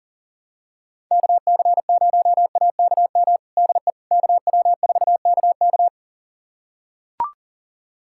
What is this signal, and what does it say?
Morse code, heard as a keyed tone, about 40 wpm, 700 Hz. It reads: KC0AKM DE KW4KK
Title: USA Callsign DE Callsign 40wpm